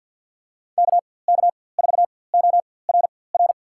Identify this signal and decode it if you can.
Morse code, heard as a keyed tone, about 53 wpm, spaced as for 24 wpm, 700 Hz. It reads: KK4ORR
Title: USA Call Signs - 6 Characters - 4x Character Spacing - 50wpm